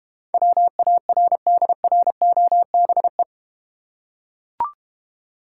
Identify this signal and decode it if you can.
Morse code, heard as a keyed tone, about 30 wpm, 700 Hz. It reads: WARDROBE
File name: Easy Words - Speed-Racing with Hints - 25wpm